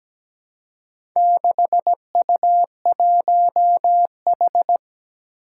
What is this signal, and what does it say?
Morse code, heard as a keyed tone, about 17 wpm, 700 Hz. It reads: 6U1H
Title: Four Challenging Letter-Number Combinations 17wpm